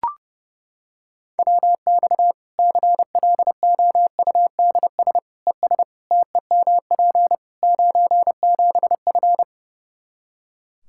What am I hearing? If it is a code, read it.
Morse code, heard as a keyed tone, about 30 wpm, 700 Hz. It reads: WX CLOUDS ES TEMP 97F
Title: Practice QSO - Exchange 2 - Info - 30wpm